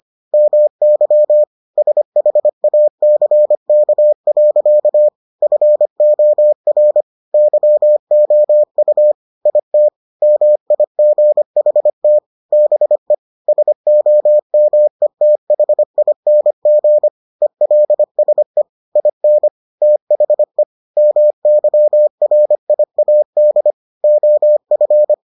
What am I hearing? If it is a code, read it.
Morse code, heard as a keyed tone, about 25 wpm, 600 Hz. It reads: MY SHACK. FOR YOU IT MIGHT BE SOMETHING ELSE IN THE MYRIAD OF